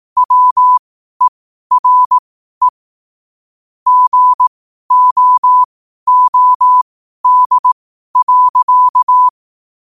Morse code, heard as a keyed tone, about 18 wpm, spaced as for 12 wpm, 1000 Hz. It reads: WERE GOOD.
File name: qotdfiles / 12wpm